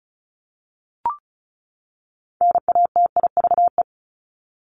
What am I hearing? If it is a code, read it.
Morse code, heard as a keyed tone, about 34 wpm, 700 Hz. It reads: NATIVE
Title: Top 1751-2000 Words 35wpm